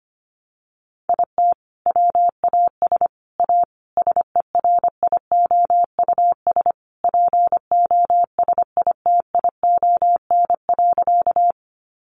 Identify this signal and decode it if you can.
Morse code, heard as a keyed tone, about 25 wpm, 700 Hz. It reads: IT WAS A SERIOUS POSITION.
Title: Sentences from Top 1100 Words 25wpm